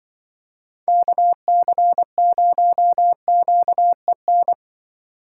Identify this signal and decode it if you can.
Morse code, heard as a keyed tone, about 25 wpm, 700 Hz. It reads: KC0QEN
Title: Speed Racing - USA Call Signs 20wpm